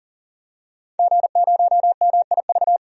Morse code, heard as a keyed tone, about 40 wpm, 700 Hz. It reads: G0MIV
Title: DXCC-100 Call Signs - 5 Characters - 40wpm